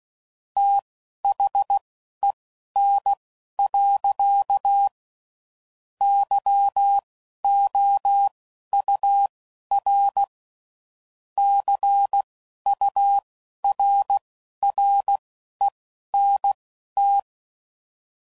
Morse code, heard as a keyed tone, about 16 wpm, spaced as for 11 wpm, 800 Hz. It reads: THEN. YOUR CURRENT
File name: CW Code Groups @10WPM / QSOs / QSO WA7QOX